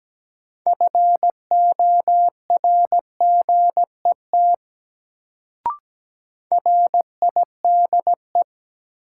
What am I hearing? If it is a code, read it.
Morse code, heard as a keyed tone, about 17 wpm, 700 Hz. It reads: FORGET RIDE
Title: Sentences from Top 1200 Words - New Word Drill Alt - 17wpm